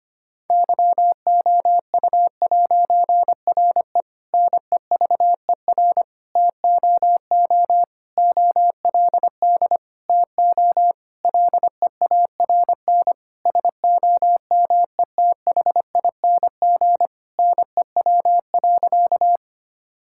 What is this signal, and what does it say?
Morse code, heard as a keyed tone, about 25 wpm, 700 Hz. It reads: YOU'RE NEVER TOO OLD TO LEARN SOMETHING NEW.
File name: practice / 25wpm / Difficult Sentences with Hints - 25wpm